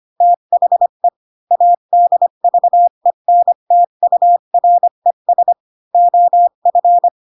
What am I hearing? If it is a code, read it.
Morse code, heard as a keyed tone, about 25 wpm, 700 Hz. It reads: THE ADVENTURES OF
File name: Pinocchio / Pinocchio ch6 20 25 ik1hge